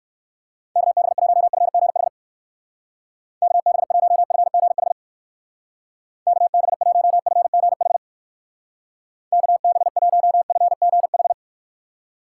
Morse code, heard as a keyed tone, about 56 wpm, 700 Hz. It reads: KB1FGH KB1FGH KB1FGH KB1FGH
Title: Speed Racing - USA 2x3 Call Signs 45wpm